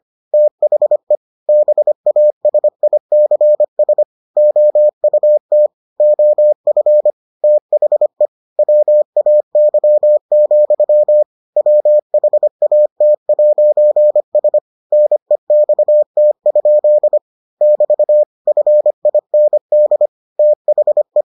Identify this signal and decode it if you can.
Morse code, heard as a keyed tone, about 25 wpm, 600 Hz. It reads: THE BASICS OUT OF THE WAY, WHAT'S NEXT? <BT> FIND THE